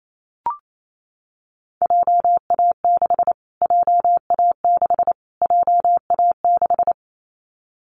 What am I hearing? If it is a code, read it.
Morse code, heard as a keyed tone, about 28 wpm, 700 Hz. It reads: JA6 JA6 JA6
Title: Callsign Prefixes for Top 25 DXCC - Repeated 3x - 28wpm